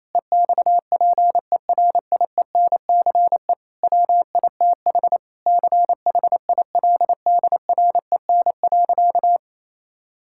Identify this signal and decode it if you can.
Morse code, heard as a keyed tone, about 28 wpm, 700 Hz. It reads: EXPERIENCE WITH CHILDREN.
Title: Sentences from Top 1000 Words - Review Part 5 - 28wpm